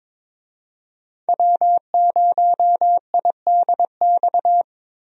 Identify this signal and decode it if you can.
Morse code, heard as a keyed tone, about 22 wpm, 700 Hz. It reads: W0IDX
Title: USA Call Signs - 5 Characters - Repeated 3x - 22wpm